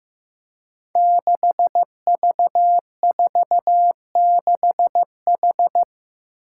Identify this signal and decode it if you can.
Morse code, heard as a keyed tone, about 15 wpm, 700 Hz. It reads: 6V46H
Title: Five Challenging Letter-Number Combinations 15wpm